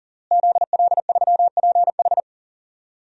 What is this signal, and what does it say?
Morse code, heard as a keyed tone, about 38 wpm, 700 Hz. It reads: ZL3PH
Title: DXCC-100 Call Signs - 5 Characters - Repeated 2x - 40wpm